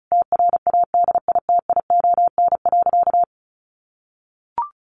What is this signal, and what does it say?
Morse code, heard as a keyed tone, about 34 wpm, 700 Hz. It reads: TRADITION.
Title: Sentences from Top 1500 Words 35wpm